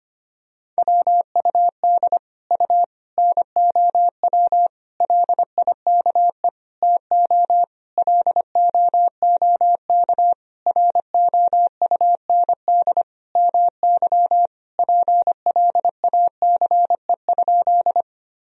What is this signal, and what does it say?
Morse code, heard as a keyed tone, about 25 wpm, 700 Hz. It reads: WUD U NOW LIKE TO LOOK ROUND MY PLACE?
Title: Sentences from Top 200 Words and Abbreviations 25wpm